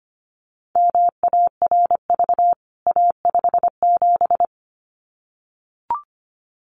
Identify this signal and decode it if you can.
Morse code, heard as a keyed tone, about 25 wpm, 700 Hz. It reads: MARV A57
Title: CWT Exchange 25wpm